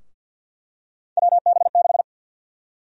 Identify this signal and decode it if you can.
Morse code, heard as a keyed tone, about 52 wpm, 700 Hz. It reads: WB6
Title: USA Call Sign Prefixes - 50wpm